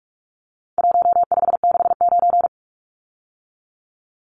Morse code, heard as a keyed tone, about 46 wpm, 700 Hz. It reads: 1569
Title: Numbers - Contest Style - 4-Digit 45wpm